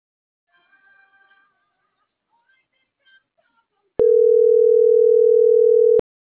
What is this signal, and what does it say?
Ringback tone (North America), 440 + 480 Hz.